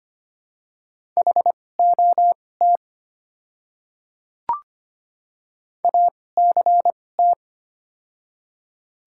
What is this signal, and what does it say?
Morse code, heard as a keyed tone, about 25 wpm, spaced as for 18 wpm, 690 Hz. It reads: HOT ACT
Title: 2x Character Spacing - Top 200 Words - 25wpm